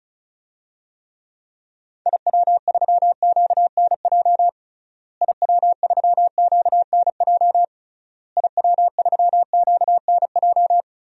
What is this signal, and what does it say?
Morse code, heard as a keyed tone, about 34 wpm, 700 Hz. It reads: IW3QNJ IW3QNJ IW3QNJ
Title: DXCC-100 Call Signs - 6 Characters - Repeated 3x - 35wpm